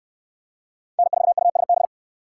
Morse code, heard as a keyed tone, about 65 wpm, 700 Hz. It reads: N4USB